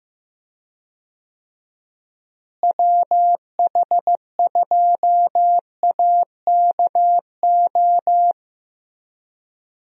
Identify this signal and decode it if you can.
Morse code, heard as a keyed tone, about 15 wpm, 700 Hz. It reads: WH2AKO
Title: USA Call Signs - 6 Characters - Repeated 2x - 15wpm